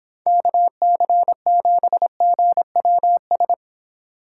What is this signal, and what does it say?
Morse code, heard as a keyed tone, about 26 wpm, 700 Hz. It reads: KC7GWS